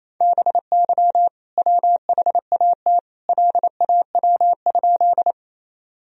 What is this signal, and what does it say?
Morse code, heard as a keyed tone, about 28 wpm, 700 Hz. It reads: BY WHAT LAW?
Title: Sentences from Top 1000 Words - Review Part 3 - 28wpm